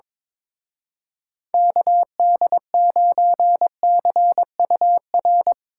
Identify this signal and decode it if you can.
Morse code, heard as a keyed tone, about 24 wpm, 700 Hz. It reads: KD9CUR
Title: Speed Racing - USA 2x3 Call Signs 22wpm